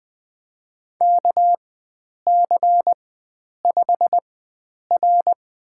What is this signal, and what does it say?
Morse code, heard as a keyed tone, about 20 wpm, spaced as for 9 wpm, 700 Hz. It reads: KC5R
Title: USA Call Signs - 4x Character Spacing - 20wpm